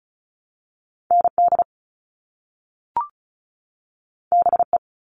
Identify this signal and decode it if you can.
Morse code, heard as a keyed tone, about 36 wpm, 700 Hz. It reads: ND BE
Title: All Cut Numbers - 2-Digit 35wpm